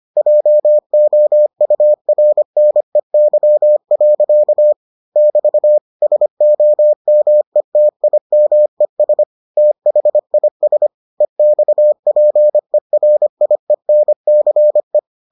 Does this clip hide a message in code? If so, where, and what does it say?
Yes: Morse code from 0.2 to 15.0 s, heard as a keyed tone, about 25 wpm, 600 Hz. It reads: JOURNEY. <BT> SOMETIMES THIS EXPERIENCE